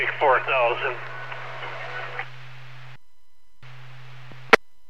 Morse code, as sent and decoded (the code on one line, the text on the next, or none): none